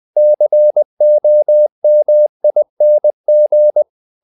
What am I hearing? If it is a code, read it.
Morse code, heard as a keyed tone, about 20 wpm, 600 Hz. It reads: COMING
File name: the old man and the sea part 2 20wpm